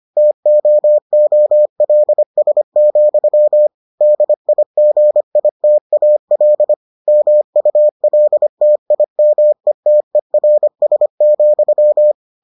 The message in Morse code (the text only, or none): TOOLS, DIGITAL MULTIMETERS,